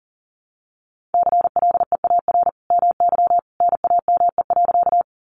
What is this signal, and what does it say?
Morse code, heard as a keyed tone, about 40 wpm, 700 Hz. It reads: CLEAR MY NAME.
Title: Sentences from Top 500 Words 40wpm